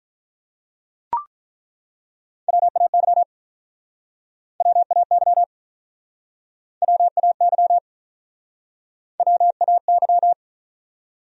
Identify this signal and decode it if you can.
Morse code, heard as a keyed tone, about 48 wpm, 700 Hz. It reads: WAY WAY WAY WAY